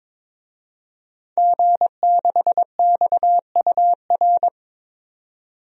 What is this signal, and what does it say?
Morse code, heard as a keyed tone, about 22 wpm, 700 Hz. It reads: G6XUR
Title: DXCC-100 Call Signs - 5 Characters - 22wpm